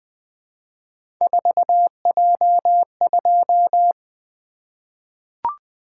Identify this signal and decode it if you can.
Morse code, heard as a keyed tone, about 20 wpm, 700 Hz. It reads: 4J2
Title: Three Letter-Number Combinations - 20wpm